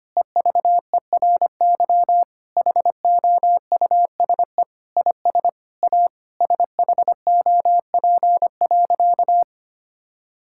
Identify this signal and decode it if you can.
Morse code, heard as a keyed tone, about 25 wpm, 700 Hz. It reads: EVERY HOUSE IS A SHOP.
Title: Sentences from Top 1600 Words 25wpm